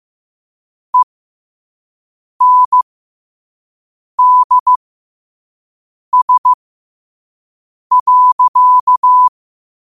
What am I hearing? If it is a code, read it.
Morse code, heard as a keyed tone, about 15 wpm, spaced as for 5 wpm, 1000 Hz. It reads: ENDS.
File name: qotdfiles / 5wpm